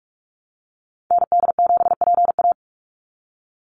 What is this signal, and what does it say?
Morse code, heard as a keyed tone, about 44 wpm, 700 Hz. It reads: ND7PA
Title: NA SOTA Activator Callsigns 45wpm